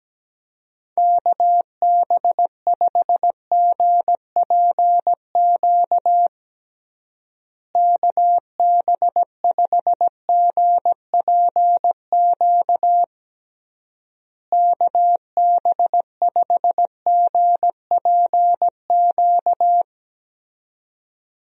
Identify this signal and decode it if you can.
Morse code, heard as a keyed tone, about 17 wpm, 700 Hz. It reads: KB5GPQ KB5GPQ KB5GPQ
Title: USA Call Signs - 6 Characters - Repeated 3x - 17wpm